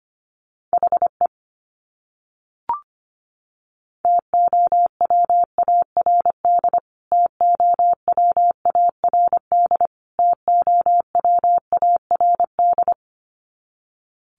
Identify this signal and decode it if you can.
Morse code, heard as a keyed tone, about 25 wpm, 700 Hz. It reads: HE TOWARD TOWARD TOWARD
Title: Top 500 Words Repeated 25wpm